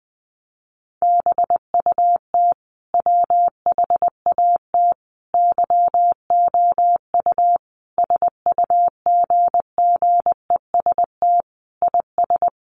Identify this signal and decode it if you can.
Morse code, heard as a keyed tone, about 19 wpm, 700 Hz. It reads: BUT WHAT YOU SUGGEST IS
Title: Sentences from Top 1300 Words 20wpm